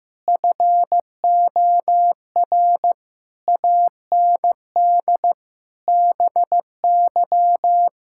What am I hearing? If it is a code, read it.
Morse code, heard as a keyed tone, about 15 wpm, 700 Hz. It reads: FOR AND BY
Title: Pangram Challenge v2 - 15wpm